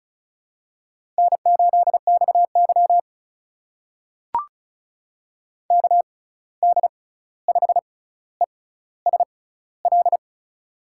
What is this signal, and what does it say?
Morse code, heard as a keyed tone, about 34 wpm, spaced as for 16 wpm, 700 Hz. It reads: N8XY KD5ESL
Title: USA Call Signs - 6x Character Spacing - 35wpm